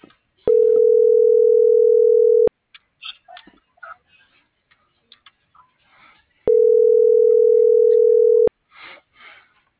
Ringback tone (North America), 440 + 480 Hz.
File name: CENTRAL BLN / out hector.tellez